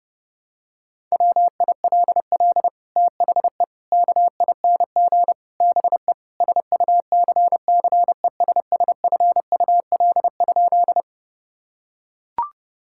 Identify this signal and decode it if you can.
Morse code, heard as a keyed tone, about 30 wpm, 700 Hz. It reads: WILL THE KING BE SUCCESSFUL?
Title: Sentences from Top 1000 Words - Review Part 6 - 30wpm